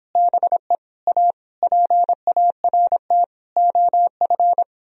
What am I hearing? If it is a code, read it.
Morse code, heard as a keyed tone, about 26 wpm, 700 Hz. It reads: BE A PART OF